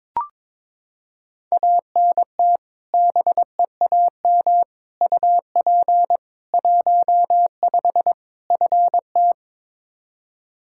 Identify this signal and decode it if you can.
Morse code, heard as a keyed tone, about 21 wpm, 700 Hz. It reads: ANT BEAM UP 15 FT
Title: Practice QSO - Exchange 2 - Info - 22wpm